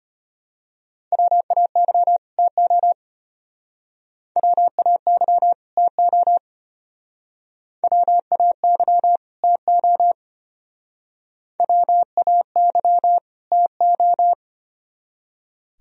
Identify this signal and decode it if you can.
Morse code, heard as a keyed tone, about 32 wpm, 700 Hz. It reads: WAY TO WAY TO WAY TO WAY TO